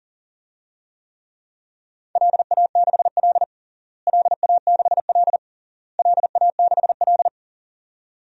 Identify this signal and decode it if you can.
Morse code, heard as a keyed tone, about 40 wpm, 700 Hz. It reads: LA6L LA6L LA6L
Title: DXCC-100 Call Signs - 4 Characters - Repeated 3x - 40wpm